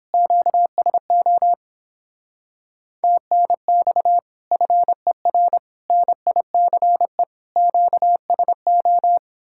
Morse code, heard as a keyed tone, about 29 wpm, 700 Hz. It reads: QSO TNX FER NICE QSO